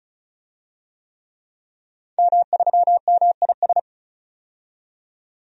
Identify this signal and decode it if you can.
Morse code, heard as a keyed tone, about 36 wpm, 700 Hz. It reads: M3MIS
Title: DXCC-100 Call Signs - 5 Characters - 35wpm